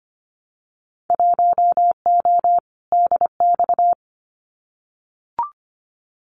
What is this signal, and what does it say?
Morse code, heard as a keyed tone, about 25 wpm, 700 Hz. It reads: 1O DX
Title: Winter Field Day Exchanges 25wpm